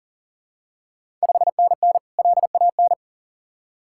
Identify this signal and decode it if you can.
Morse code, heard as a keyed tone, about 38 wpm, 690 Hz. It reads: 5NN LAN
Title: Ontario QSO Party Exchange 40wpm